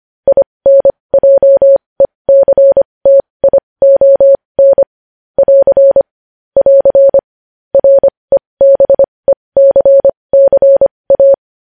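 Morse code, heard as a keyed tone, about 25 wpm, spaced as for 20 wpm, 550 Hz. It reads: INJECTION <AR> <AR> REBECCA
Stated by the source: Morse code news headlines practice at 20 wpm.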